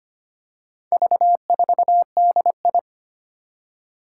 Morse code, heard as a keyed tone, about 25 wpm, 700 Hz. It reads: V4DI
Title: Four Challenging Letter-Number Combinations 25wpm